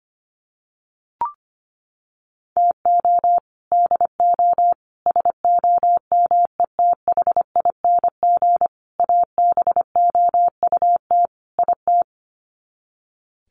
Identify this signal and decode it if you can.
Morse code, heard as a keyed tone, about 25 wpm, 700 Hz. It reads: TO DO SOMETHING ABOUT IT
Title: Sets of 5 Words 25wpm